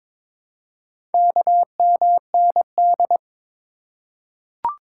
Morse code, heard as a keyed tone, about 23 wpm, 700 Hz. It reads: KMND
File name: practice / 22wpm / Four Letter-Number Combinations - 22wpm